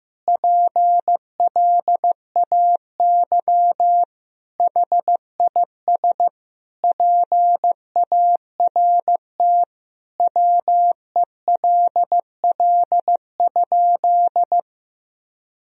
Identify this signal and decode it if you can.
Morse code, heard as a keyed tone, about 15 wpm, 700 Hz. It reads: PLAY HIS PART WELL?
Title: Sentences from Top 1000 Words - Review Part 1 - 15wpm